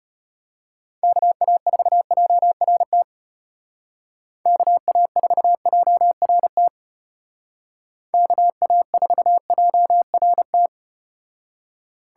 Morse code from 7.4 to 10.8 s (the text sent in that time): KA4JRT